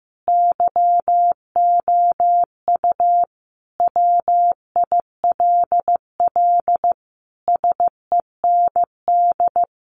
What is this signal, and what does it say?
Morse code, heard as a keyed tone, about 15 wpm, 700 Hz. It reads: YOU WILL SEND
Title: Sentences from Top 1500 Words 15wpm